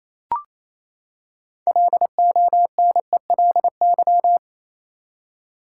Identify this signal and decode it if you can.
Morse code, heard as a keyed tone, about 28 wpm, 700 Hz. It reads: LONELY
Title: Six Letter Words - 28wpm